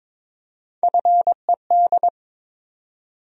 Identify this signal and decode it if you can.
Morse code, heard as a keyed tone, about 22 wpm, 700 Hz. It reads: FED
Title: USA Call Sign Postfixes - 3 Letters 22wpm